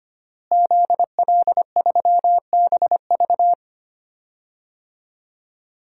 Morse code, heard as a keyed tone, about 25 wpm, 700 Hz. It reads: ZL3BV
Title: DXCC-100 Call Signs - 5 Characters - Repeated 3x - 25wpm